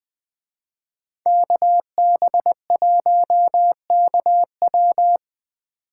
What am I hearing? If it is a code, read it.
Morse code, heard as a keyed tone, about 20 wpm, 700 Hz. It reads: KB1KW
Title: Speed Racing - USA Call Signs 17wpm